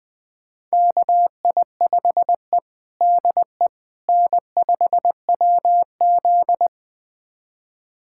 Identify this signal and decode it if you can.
Morse code, heard as a keyed tone, about 21 wpm, 700 Hz. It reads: KI5E DE N5WZ
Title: USA Callsign DE Callsign - 4 Characters - 20wpm